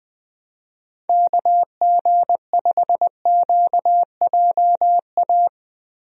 Morse code, heard as a keyed tone, about 21 wpm, 700 Hz. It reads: KG5QJA
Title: USA Call Signs - 6 Characters - Repeated 3x - 20wpm